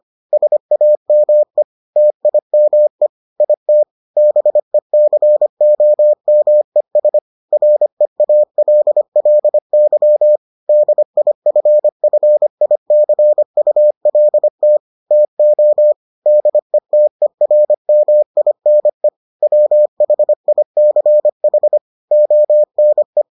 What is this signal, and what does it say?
Morse code, heard as a keyed tone, about 25 wpm, 600 Hz. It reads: SAME TIME IT BECOMES REALLY DIFFICULT TO DETERMINE WHICH ONE